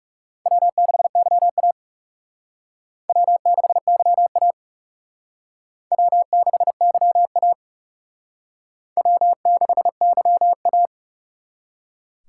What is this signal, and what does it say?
Morse code, heard as a keyed tone, about 38 wpm, 700 Hz. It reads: W6YA W6YA W6YA W6YA
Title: Speed Racing - USA Call Signs 30wpm